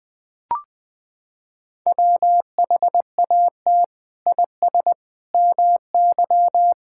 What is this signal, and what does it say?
Morse code, heard as a keyed tone, about 20 wpm, 700 Hz. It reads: WHAT IS MY
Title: Sentences from Top 1000 Words - Review Part 6 - 20wpm